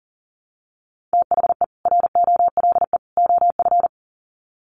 Morse code, heard as a keyed tone, about 40 wpm, 700 Hz. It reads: THE ROLE OF
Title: Sets of 3 Words 40wpm